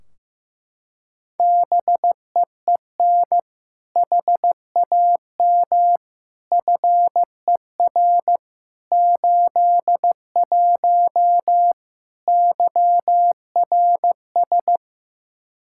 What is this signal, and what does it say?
Morse code, heard as a keyed tone, about 15 wpm, 700 Hz. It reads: BEEN HAM FER 81 YRS
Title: Practice QSO - Exchange 3 - More Info - 15wpm